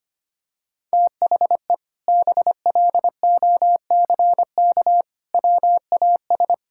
Morse code, heard as a keyed tone, about 25 wpm, 700 Hz. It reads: THE BLOCK WAS
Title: Sentences from Top 1600 Words 25wpm